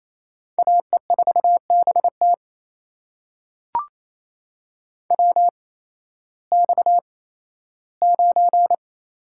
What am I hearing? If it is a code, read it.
Morse code, heard as a keyed tone, about 28 wpm, spaced as for 15 wpm, 700 Hz. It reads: AE4BT WX9
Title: USA Call Signs - 5 Characters - 8x Character Spacing - 28wpm